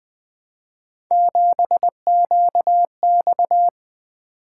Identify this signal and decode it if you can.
Morse code, heard as a keyed tone, about 20 wpm, 700 Hz. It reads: 7QX
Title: Three Letter-Number Combinations - 20wpm